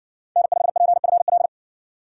Morse code, heard as a keyed tone, about 55 wpm, 700 Hz. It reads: N5PFL